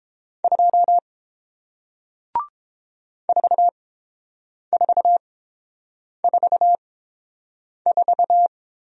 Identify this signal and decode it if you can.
Morse code, heard as a keyed tone, about 30 wpm, spaced as for 7 wpm, 700 Hz. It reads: 2 4444